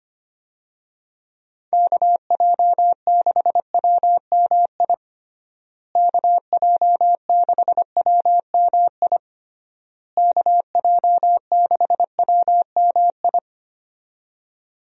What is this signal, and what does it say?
Morse code, heard as a keyed tone, about 25 wpm, 700 Hz. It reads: KJ6WMI KJ6WMI KJ6WMI
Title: USA Call Signs - 6 Characters - Repeated 3x - 25wpm